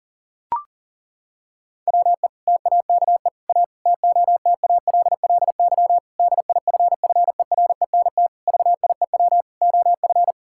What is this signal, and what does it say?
Morse code, heard as a keyed tone, about 37 wpm, 700 Hz. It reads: WE TAKE A TOTALLY DIFFERENT VIEW OF
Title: Sentences from Top 1600 Words 40wpm